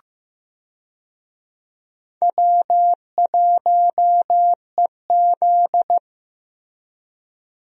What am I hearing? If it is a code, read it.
Morse code, heard as a keyed tone, about 15 wpm, 700 Hz. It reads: W1EZ
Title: USA Call Signs - 4 Characters - Repeated 2x - 15wpm